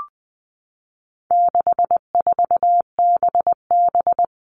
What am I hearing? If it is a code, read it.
Morse code, heard as a keyed tone, about 20 wpm, 700 Hz. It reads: 64BB
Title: All Cut Numbers - 4-Digit 20wpm